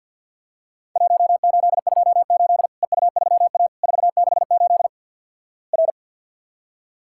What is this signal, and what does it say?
Morse code, heard as a keyed tone, about 52 wpm, 700 Hz. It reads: 1828 EU3A 468 R
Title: Worked All Europe DX Contest - QTC Exchange - 50wpm